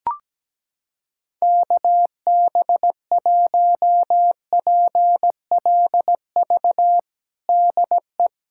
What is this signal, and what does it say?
Morse code, heard as a keyed tone, about 17 wpm, 700 Hz. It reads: KB1PLV DE
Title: USA Callsign DE Callsign - 6 Characters - 17wpm